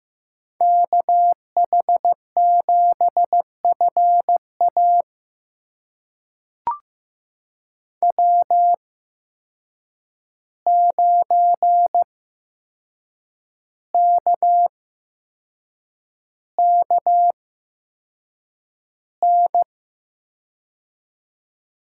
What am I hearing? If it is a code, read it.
Morse code, heard as a keyed tone, about 15 wpm, spaced as for 6 wpm, 700 Hz. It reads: KH7FA W9KKN
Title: USA Call Signs - 5 Characters - 8x Character Spacing - 15wpm